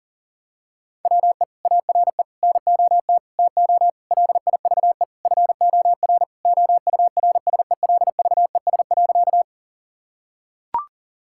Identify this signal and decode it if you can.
Morse code, heard as a keyed tone, about 40 wpm, 700 Hz. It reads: WE ARE NOT TO LIVE FOR OURSELVES.
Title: Sentences from Top 1600 Words 40wpm